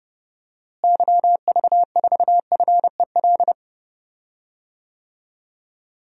Morse code, heard as a keyed tone, about 32 wpm, 700 Hz. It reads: YV4FEL
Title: DXCC-100 Call Signs - 6 Characters - Repeated 3x - 30wpm